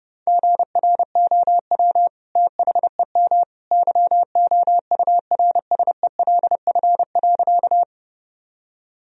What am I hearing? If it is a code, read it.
Morse code, heard as a keyed tone, about 32 wpm, 700 Hz. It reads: GROW THEM YOURSELF.
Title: Sentences from Top 1000 Words - Review Part 1 - 30wpm